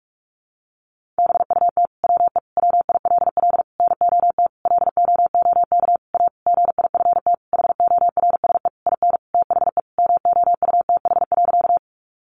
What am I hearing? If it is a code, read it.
Morse code, heard as a keyed tone, about 44 wpm, 700 Hz. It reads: BUT WE WILL NOT LOOK A GIFT HORSE IN THE MOUTH.
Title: Sentences from Top 1500 Words 45wpm